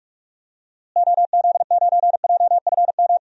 Morse code, heard as a keyed tone, about 42 wpm, 700 Hz. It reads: OZ9JFM
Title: DXCC-100 Call Signs - 6 Characters - Repeated 2x - 45wpm